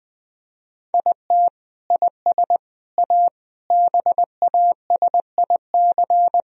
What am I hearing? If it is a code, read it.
Morse code, heard as a keyed tone, about 20 wpm, 700 Hz. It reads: IT IS A BASIC
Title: Sentences from Top 1000 Words - Review Part 3 - 20wpm